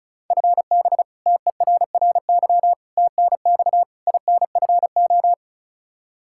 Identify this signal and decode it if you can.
Morse code, heard as a keyed tone, about 36 wpm, 700 Hz. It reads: FB TERRY TNX INFO
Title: Practice QSO - Exchange 3 - More Info - 35wpm